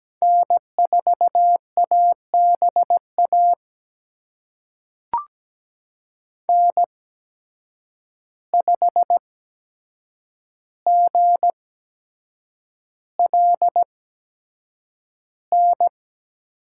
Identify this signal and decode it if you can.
Morse code, heard as a keyed tone, about 17 wpm, spaced as for 7 wpm, 700 Hz. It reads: N4ABA N5GLN
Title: USA Call Signs - 8x Character Spacing - 17wpm